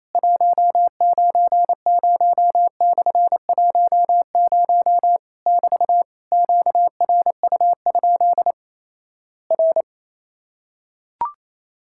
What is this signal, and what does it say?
Morse code, heard as a keyed tone, about 27 wpm, 700 Hz. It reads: 190/10 <BT> QRU? R
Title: Worked All Europe DX Contest - QTC Exchange - 28wpm